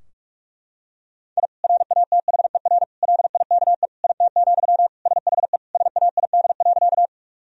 Morse code, heard as a keyed tone, about 42 wpm, 700 Hz. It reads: I RATHER LIKE IT, SHE SAID.
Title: Sentences from Top 1000 Words - Review Part 4 - 45wpm